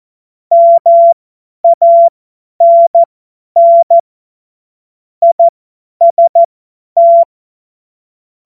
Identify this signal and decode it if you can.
Morse code, heard as a keyed tone, about 14 wpm, spaced as for 10 wpm, 680 Hz. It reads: MANN IST
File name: Morsekurs / Witze / Witz 02 April 2024 50 F70